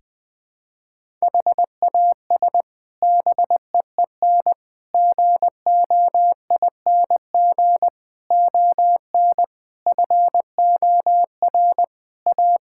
Morse code, heard as a keyed tone, about 20 wpm, 700 Hz. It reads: HAS BEEN GOING ON FOR A
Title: Sets of 8 Words - 20wpm